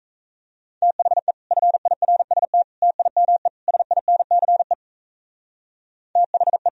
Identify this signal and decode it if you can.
Morse code, heard as a keyed tone, about 34 wpm, 700 Hz. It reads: THE FIRST TIME SINCE THE
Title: Speed-Racing - Sets of 4 Words 28wpm